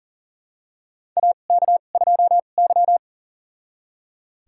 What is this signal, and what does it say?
Morse code, heard as a keyed tone, about 40 wpm, spaced as for 29 wpm, 700 Hz. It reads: AK2Y
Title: USA Call Signs - 4 Characters - 2x Character Spacing - 40wpm